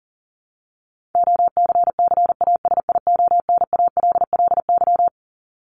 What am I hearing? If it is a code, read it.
Morse code, heard as a keyed tone, about 40 wpm, 700 Hz. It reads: OCCASIONALLY
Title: Top 1001-1250 Words 40wpm